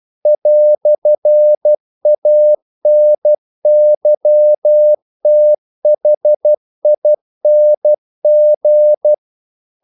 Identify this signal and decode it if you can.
Morse code, heard as a keyed tone, about 12 wpm, 600 Hz. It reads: "ANYTHING
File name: CW / Practice / EnglishQuotations0312